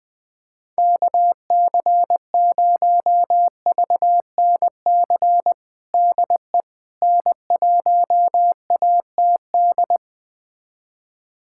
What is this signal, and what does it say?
Morse code, heard as a keyed tone, about 21 wpm, 700 Hz. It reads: KC0VNC DE N1ATD
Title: USA Callsign DE Callsign 20wpm